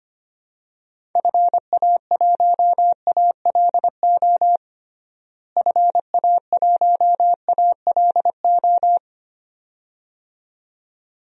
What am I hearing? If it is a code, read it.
Morse code, heard as a keyed tone, about 25 wpm, 700 Hz. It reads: FA1ALO FA1ALO
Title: DXCC-100 Call Signs - 6 Characters - Repeated 2x - 25wpm